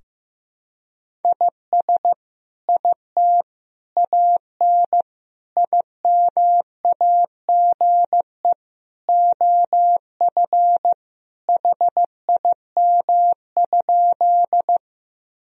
Morse code, heard as a keyed tone, about 15 wpm, 700 Hz. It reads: IS IT AN IMAGE OF HIM?
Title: Sentences from Top 1000 Words - Review Part 4 - 15wpm